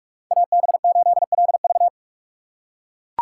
Morse code, heard as a keyed tone, about 43 wpm, 700 Hz. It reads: AB8LV